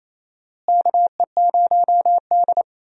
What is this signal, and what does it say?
Morse code, heard as a keyed tone, about 28 wpm, 700 Hz. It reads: KE0D
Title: USA Call Signs - 4 Characters - Repeated 3x - 28wpm